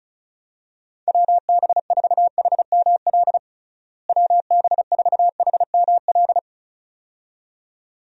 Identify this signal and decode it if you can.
Morse code, heard as a keyed tone, about 34 wpm, 700 Hz. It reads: WB4HML WB4HML
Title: USA Call Signs - 6 Characters - Repeated 2x - 35wpm